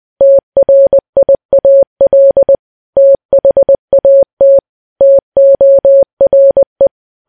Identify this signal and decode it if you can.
Morse code, heard as a keyed tone, about 20 wpm, 550 Hz. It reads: TRIAL THAT TORE
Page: news-headlines-in-morse-code-at-20-wpm